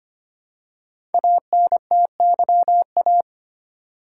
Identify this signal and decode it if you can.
Morse code, heard as a keyed tone, about 25 wpm, 700 Hz. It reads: ANTYA
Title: Code Groups - 5 Letter:Numbers - 25wpm